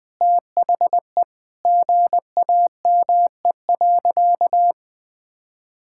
Morse code, heard as a keyed tone, about 21 wpm, 700 Hz. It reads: THE GAME.
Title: Sentences from Top 1000 Words - Review Part 6 - 20wpm